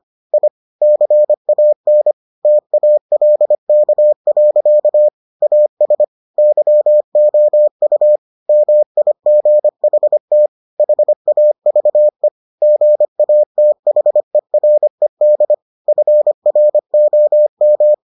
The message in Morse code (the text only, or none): I CAN TALK. AS YOU MIGHT HAVE GATHERED FROM